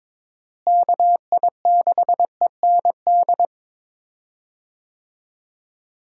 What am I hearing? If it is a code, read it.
Morse code, heard as a keyed tone, about 22 wpm, 700 Hz. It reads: KI6END
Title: USA Call Signs - 6 Characters - Repeated 3x - 22wpm